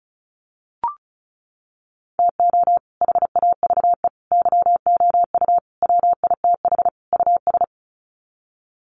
Morse code, heard as a keyed tone, about 34 wpm, 700 Hz. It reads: TO HAVE YOU WITH US
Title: Sets of 5 Words 35wpm